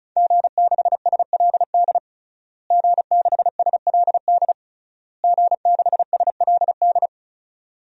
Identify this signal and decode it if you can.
Morse code, heard as a keyed tone, about 36 wpm, 700 Hz. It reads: G6SLD G6SLD G6SLD
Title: DXCC-100 Call Signs - 5 Characters - Repeated 3x - 35wpm